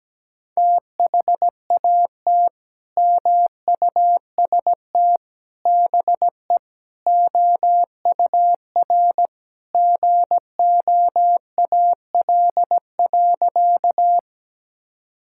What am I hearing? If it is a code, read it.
Morse code, heard as a keyed tone, about 17 wpm, 700 Hz. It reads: THAT MUST BE OUR GOAL.
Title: Sentences from Top 1000 Words - Review Part 5 - 17wpm